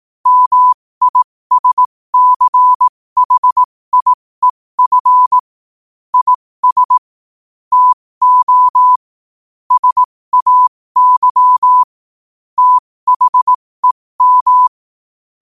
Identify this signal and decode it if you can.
Morse code, heard as a keyed tone, about 18 wpm, spaced as for 15 wpm, 1000 Hz. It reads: MISCHIEF IS TO SAY THEM
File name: qotdfiles / 15wpm